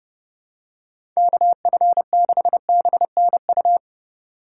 Morse code, heard as a keyed tone, about 30 wpm, 700 Hz. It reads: KF6BNU
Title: USA Call Signs - 3x Character Spacing - 30wpm